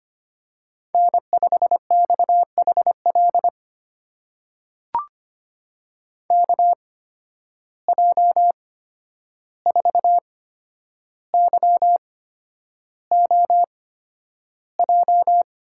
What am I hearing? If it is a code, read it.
Morse code, heard as a keyed tone, about 25 wpm, spaced as for 10 wpm, 700 Hz. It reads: N5XHL KJ4YOJ
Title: USA Call Signs - 8x Character Spacing - 25wpm